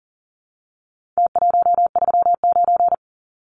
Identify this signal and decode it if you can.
Morse code, heard as a keyed tone, about 40 wpm, 700 Hz. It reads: T139
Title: Numbers - Contest Style - 4-Digit 40wpm